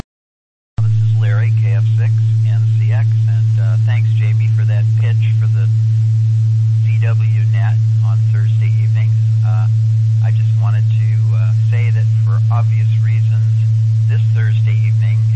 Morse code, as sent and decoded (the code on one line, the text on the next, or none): none